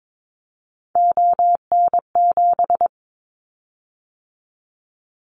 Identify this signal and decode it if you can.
Morse code, heard as a keyed tone, about 22 wpm, 700 Hz. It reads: ON7
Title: Callsign Prefixes for Top 50 DXCC - 22wpm